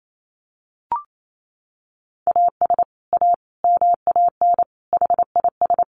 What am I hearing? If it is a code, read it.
Morse code, heard as a keyed tone, about 28 wpm, 700 Hz. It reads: AS A MAN HIS
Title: Sentences from Top 1200 Words 28wpm